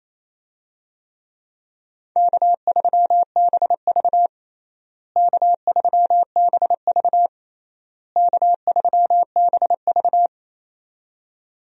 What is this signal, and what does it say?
Morse code, heard as a keyed tone, about 28 wpm, 700 Hz. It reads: K3BV K3BV K3BV
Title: USA Call Signs - 4 Characters - Repeated 3x - 28wpm